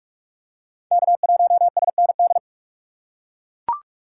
Morse code, heard as a keyed tone, about 41 wpm, 700 Hz. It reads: K1SND
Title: K1USN SST QSO Practice - 45wpm